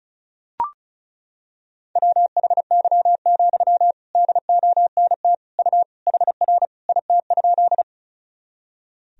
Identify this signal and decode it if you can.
Morse code, heard as a keyed tone, about 34 wpm, 700 Hz. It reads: WHY, DONT U HR IT?
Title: Sentences from Top 300 Words and Abbreviations 35wpm